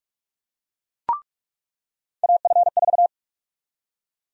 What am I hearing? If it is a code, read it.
Morse code, heard as a keyed tone, about 48 wpm, 700 Hz. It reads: AF4
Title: USA Call Sign Prefixes - 45wpm